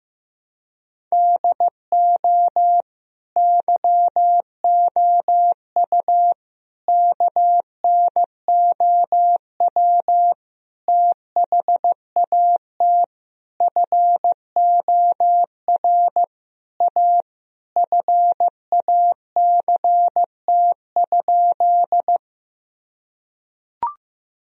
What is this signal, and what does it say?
Morse code, heard as a keyed tone, about 15 wpm, 700 Hz. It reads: DO YOU KNOW THAT FOR A FACT?
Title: Sentences from Top 1000 Words - Review Part 3 - 15wpm